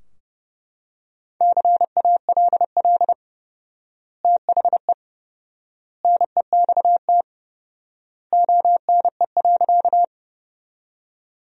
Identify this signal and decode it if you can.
Morse code, heard as a keyed tone, about 30 wpm, 700 Hz. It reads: CALL THE NEXT ONE.
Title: 4x Word-Spacing - Sentences from Top 300 Words - 30wpm